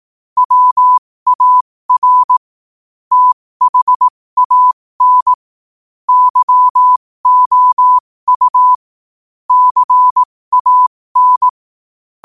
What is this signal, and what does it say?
Morse code, heard as a keyed tone, about 18 wpm, spaced as for 15 wpm, 1000 Hz. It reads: WAR THAN YOU CAN
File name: qotdfiles / 15wpm